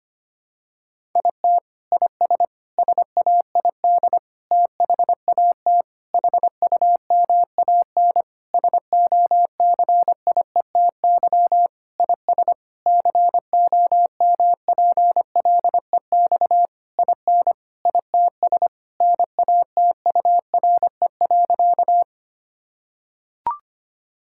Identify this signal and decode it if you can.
Morse code, heard as a keyed tone, about 25 wpm, 700 Hz. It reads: IT IS SAID THAT HUMAN SOCIETY IS COMPLEX IN ITS NATURE.
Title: Sentences from Top 1000 Words - Review Part 6 - 25wpm